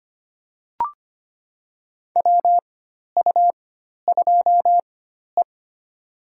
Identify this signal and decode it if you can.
Morse code, heard as a keyed tone, about 25 wpm, spaced as for 12 wpm, 700 Hz. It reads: WU2E
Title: USA Call Signs - 4x Character Spacing - 25wpm